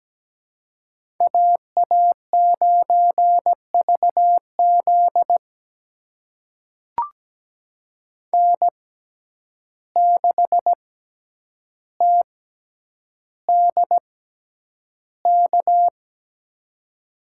Morse code, heard as a keyed tone, about 17 wpm, spaced as for 9 wpm, 700 Hz. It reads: AA9VZ N6TDK
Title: USA Call Signs - 5 Characters - 6x Character Spacing - 17wpm